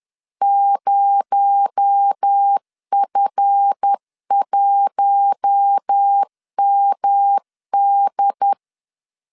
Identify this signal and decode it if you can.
Morse code, heard as a keyed tone, about 10 wpm, 800 Hz. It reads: 0F1MD
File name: LEZIONI / 7 GF9Z / 7 GF9Z 10wpm 2611